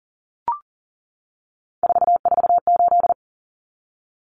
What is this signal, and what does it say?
Morse code, heard as a keyed tone, about 40 wpm, 700 Hz. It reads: 448
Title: Common Cut Numbers - 3-Digit 40wpm